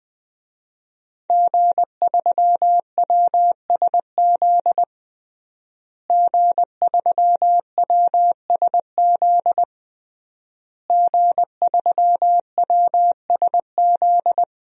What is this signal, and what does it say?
Morse code, heard as a keyed tone, about 20 wpm, 700 Hz. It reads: G3WSZ G3WSZ G3WSZ
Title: DXCC-100 Call Signs - 5 Characters - Repeated 3x - 20wpm